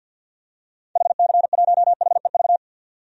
Morse code, heard as a keyed tone, about 52 wpm, 700 Hz. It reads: HC1HEV
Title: DXCC-100 Call Signs - 6 Characters - 50wpm